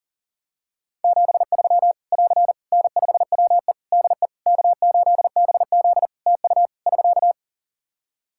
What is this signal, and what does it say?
Morse code, heard as a keyed tone, about 40 wpm, 700 Hz. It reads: 73 <AR> N5WE DE K8BZ TU <SK>
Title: K1USN SST QSO Practice 40wpm